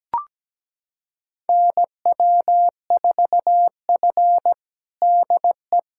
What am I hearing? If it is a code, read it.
Morse code, heard as a keyed tone, about 17 wpm, 700 Hz. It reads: NW4F DE
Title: USA Callsign DE Callsign - 4 Characters - 17wpm